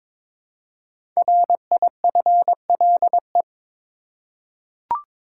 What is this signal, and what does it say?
Morse code, heard as a keyed tone, about 22 wpm, 700 Hz. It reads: RIFLE
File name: 5 Letter Words - First 3 Letters Given - 22wpm